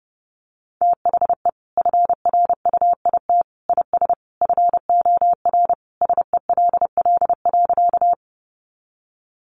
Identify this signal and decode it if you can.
Morse code, heard as a keyed tone, about 30 wpm, 700 Hz. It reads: THE FRUIT IS FOR SELL.
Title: Sentences from Top 1300 Words 30wpm